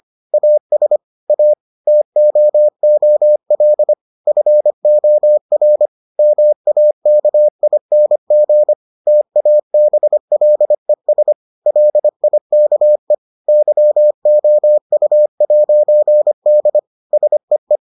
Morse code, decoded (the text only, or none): AS A TOOL FOR MAKING TABLES LIKE YOU'D SEE